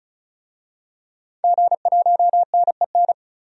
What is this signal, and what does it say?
Morse code, heard as a keyed tone, about 36 wpm, 700 Hz. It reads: G1NEN
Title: DXCC-100 Call Signs - 5 Characters - 35wpm